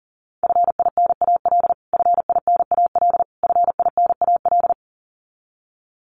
Morse code, heard as a keyed tone, about 40 wpm, 700 Hz. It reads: FINAL FINAL FINAL
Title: Top 500 Words Repeated 40wpm